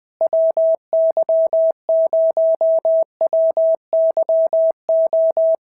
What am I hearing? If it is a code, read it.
Morse code, heard as a keyed tone, about 20 wpm, 650 Hz. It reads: WY0WYO
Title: SOTA QSO Practice 20wpm